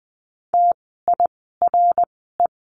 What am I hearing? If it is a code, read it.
Morse code, heard as a keyed tone, about 20 wpm, spaced as for 14 wpm, 700 Hz. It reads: TIRE
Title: 061 Lesson 11f - Letter h - Words 20wpm